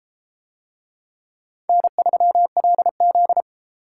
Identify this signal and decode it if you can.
Morse code, heard as a keyed tone, about 28 wpm, 700 Hz. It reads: N3LZ